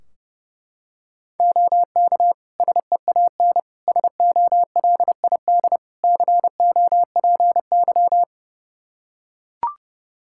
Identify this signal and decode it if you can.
Morse code, heard as a keyed tone, about 30 wpm, 700 Hz. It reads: OK SEAN SOLID COPY
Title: Practice QSO - Exchange 3 - More Info - 30wpm